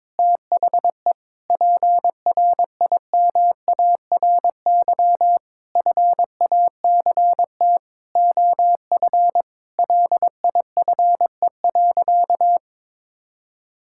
Morse code, heard as a keyed tone, about 21 wpm, 700 Hz. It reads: THE PRIMARY FACT OF LIFE.
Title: Sentences from Top 1000 Words - Review Part 5 - 22wpm